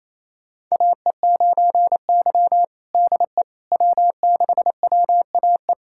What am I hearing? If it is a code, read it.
Morse code, heard as a keyed tone, about 28 wpm, 700 Hz. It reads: AE9Y DE W6WAE
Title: USA Callsign DE Callsign 28wpm